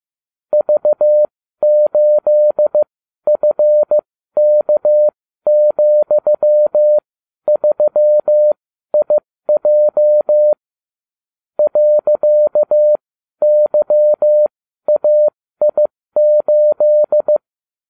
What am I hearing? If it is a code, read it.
Morse code, heard as a keyed tone, about 15 wpm, spaced as for 12 wpm, 600 Hz. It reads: V8FK,3IJ .YAI8